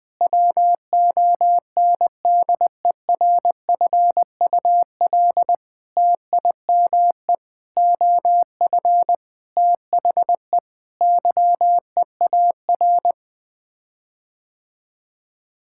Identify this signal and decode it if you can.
Morse code, heard as a keyed tone, about 20 wpm, 700 Hz. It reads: WONDERFUL TIME OF THE YEAR
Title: Sets of 8 Words - 20wpm